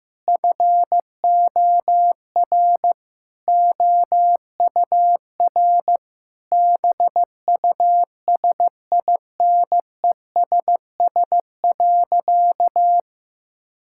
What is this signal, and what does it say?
Morse code, heard as a keyed tone, about 14 wpm, 700 Hz. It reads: FOR OUR BUSINESS.
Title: Sentences from Top 1000 Words - Review Part 6 - 15wpm